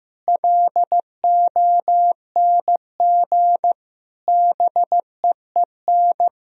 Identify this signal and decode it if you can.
Morse code, heard as a keyed tone, about 15 wpm, 700 Hz. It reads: LONG BEEN
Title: Sentences from Top 1000 Words - Review Part 3 - 15wpm